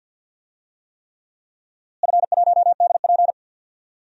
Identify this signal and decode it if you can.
Morse code, heard as a keyed tone, about 50 wpm, 700 Hz. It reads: F1DP
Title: DXCC-100 Call Signs - 4 Characters - Repeated 3x - 50wpm